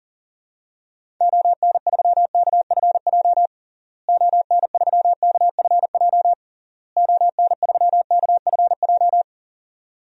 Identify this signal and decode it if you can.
Morse code, heard as a keyed tone, about 40 wpm, 700 Hz. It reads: ON3KFJ ON3KFJ ON3KFJ
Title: DXCC-100 Call Signs - 6 Characters - Repeated 3x - 40wpm